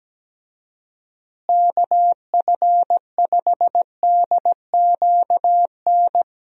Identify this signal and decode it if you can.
Morse code, heard as a keyed tone, about 17 wpm, 700 Hz. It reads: KF5DQN
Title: USA Call Signs - 6 Characters - Repeated 2x - 17wpm